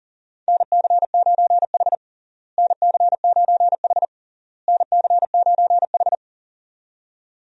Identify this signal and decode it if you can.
Morse code, heard as a keyed tone, about 40 wpm, 700 Hz. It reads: NC9H NC9H NC9H
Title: USA Call Signs - 4 Characters - Repeated 3x - 40wpm